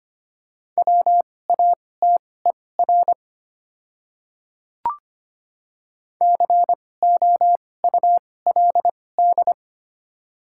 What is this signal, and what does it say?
Morse code, heard as a keyed tone, about 25 wpm, spaced as for 18 wpm, 700 Hz. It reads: WATER COULD
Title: 2x Character Spacing - Top 100 Words - 25wpm